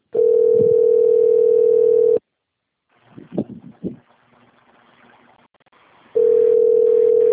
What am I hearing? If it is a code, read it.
Ringback tone (North America), 440 + 480 Hz.
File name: rg-1600-unknown- - - .2464